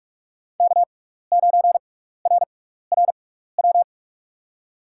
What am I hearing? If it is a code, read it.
Morse code, heard as a keyed tone, about 46 wpm, spaced as for 16 wpm, 700 Hz. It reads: K9RRW
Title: USA Call Signs - 5 Characters - 6x Character Spacing - 45wpm